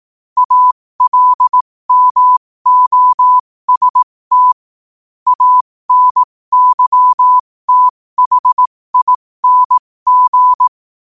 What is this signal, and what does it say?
Morse code, heard as a keyed tone, about 18 wpm, spaced as for 15 wpm, 1000 Hz. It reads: ALMOST ANYTHING
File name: qotdfiles / 15wpm